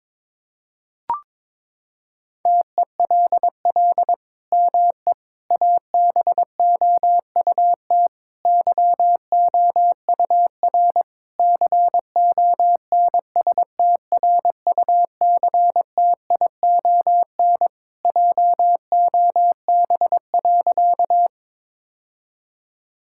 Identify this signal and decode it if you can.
Morse code, heard as a keyed tone, about 21 wpm, 700 Hz. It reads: TELL ME ABOUT YOUR CONSTRUCTION JOB.
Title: Sentences from Top 1600 Words 22wpm